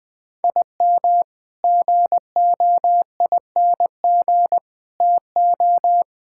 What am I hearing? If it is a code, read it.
Morse code, heard as a keyed tone, about 21 wpm, 700 Hz. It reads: IM GOING TO
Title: Sentences with 3-Word Phrases - 20wpm